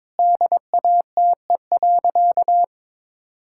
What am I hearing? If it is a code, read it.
Morse code, heard as a keyed tone, about 22 wpm, 700 Hz. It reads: DATE.
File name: Sentences with 7-Word Phrases - 22wpm